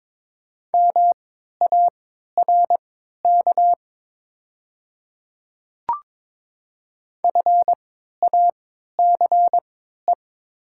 Morse code, heard as a keyed tone, about 22 wpm, spaced as for 13 wpm, 700 Hz. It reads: MARK FACE
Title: 3x Character Spacing - Top 500 Words - 22wpm